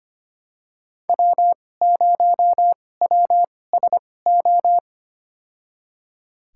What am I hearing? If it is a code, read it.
Morse code, heard as a keyed tone, about 25 wpm, spaced as for 18 wpm, 700 Hz. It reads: W0WSO
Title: USA Call Signs - 5 Characters - 2x Character Spacing - 25wpm